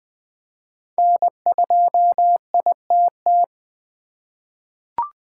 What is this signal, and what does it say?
Morse code, heard as a keyed tone, about 21 wpm, 700 Hz. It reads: N2ITT
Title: USA Call Signs - 5 Characters - 8x Character Spacing - 20wpm